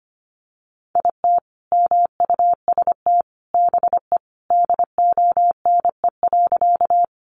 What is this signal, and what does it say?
Morse code, heard as a keyed tone, about 25 wpm, 700 Hz. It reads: IT MUST BE DONE.
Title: Sentences from Top 500 Words 25wpm